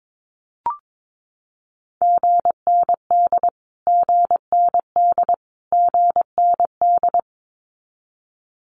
Morse code, heard as a keyed tone, about 22 wpm, 700 Hz. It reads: GND GND GND
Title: Abbreviations Repeated 3x - 22wpm